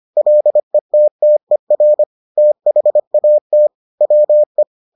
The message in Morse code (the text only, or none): LETTER THAT WE